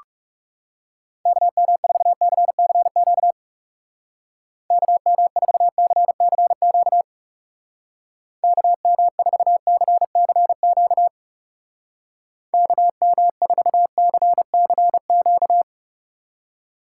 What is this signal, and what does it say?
Morse code, heard as a keyed tone, about 37 wpm, 700 Hz. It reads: KM4CCQ KM4CCQ KM4CCQ KM4CCQ
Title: Speed Racing - USA 2x3 Call Signs 30wpm